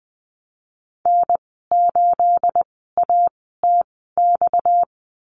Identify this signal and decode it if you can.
Morse code, heard as a keyed tone, about 19 wpm, spaced as for 14 wpm, 700 Hz. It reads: N8ATX
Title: 231 Lesson 37g - Letter x - Call Signs 20wpm